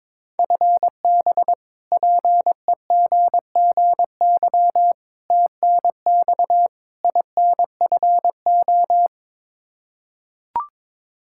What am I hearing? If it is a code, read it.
Morse code, heard as a keyed tone, about 22 wpm, 700 Hz. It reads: FB PEGGY TNX INFO
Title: Practice QSO - Exchange 3 - More Info - 22wpm